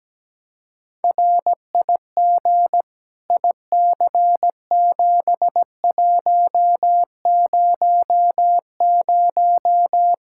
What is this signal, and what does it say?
Morse code, heard as a keyed tone, about 17 wpm, 700 Hz. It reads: RIG IC7100
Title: K1USN SST QSO Practice 17wpm